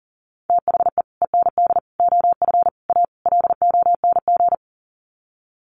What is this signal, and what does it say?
Morse code, heard as a keyed tone, about 40 wpm, 700 Hz. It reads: THE END OF A LONG
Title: Sets of 5 Words Encore2 40wpm